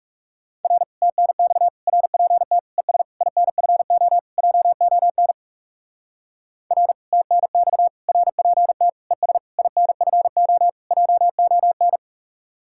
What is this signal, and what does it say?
Morse code, heard as a keyed tone, about 40 wpm, 700 Hz. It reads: R TNX RPT ES INFO JON R TNX RPT ES INFO JON